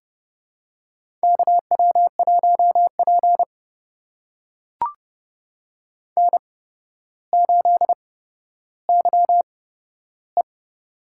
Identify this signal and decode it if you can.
Morse code, heard as a keyed tone, about 32 wpm, spaced as for 13 wpm, 700 Hz. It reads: KW1P N8YE
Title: USA Call Signs - 4 Characters - 8x Character Spacing - 30wpm